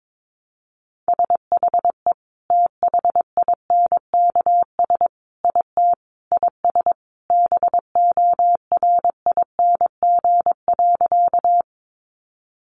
Morse code, heard as a keyed tone, about 22 wpm, 700 Hz. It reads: SHE THINKS IT IS BORING.
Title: Easy Phrases 22wpm